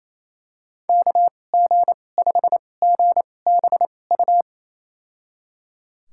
Morse code, heard as a keyed tone, about 28 wpm, spaced as for 20 wpm, 700 Hz. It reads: KG5GBU
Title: USA Call Signs - 6 Characters - 2x Character Spacing - 28wpm